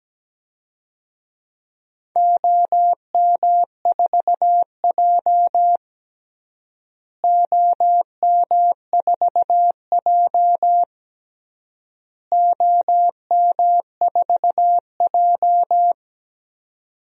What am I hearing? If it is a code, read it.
Morse code, heard as a keyed tone, about 17 wpm, 700 Hz. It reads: OM4J OM4J OM4J
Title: DXCC-100 Call Signs - 4 Characters - Repeated 3x - 17wpm